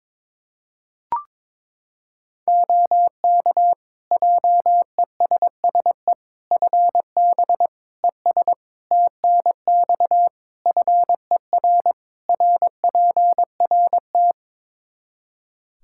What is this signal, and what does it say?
Morse code, heard as a keyed tone, about 22 wpm, 700 Hz. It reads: OK JESSE FB ES TNX FER RPRT
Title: Practice QSO - Exchange 2 - Info - 22wpm